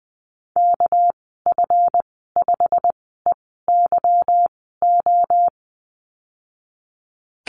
Morse code, heard as a keyed tone, about 20 wpm, 700 Hz. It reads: K F 5 E Y O
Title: 238 Lesson 38g - Letter q - Call Signs 20wpm